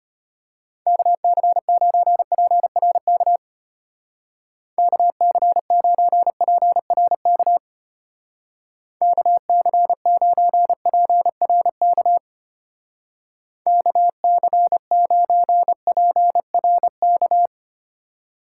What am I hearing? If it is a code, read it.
Morse code, heard as a keyed tone, about 32 wpm, 700 Hz. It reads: KC9PRK KC9PRK KC9PRK KC9PRK